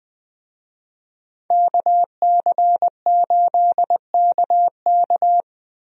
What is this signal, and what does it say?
Morse code, heard as a keyed tone, about 20 wpm, 700 Hz. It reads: KC8KK
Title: USA Call Signs - 5 Characters - Repeated 2x - 20wpm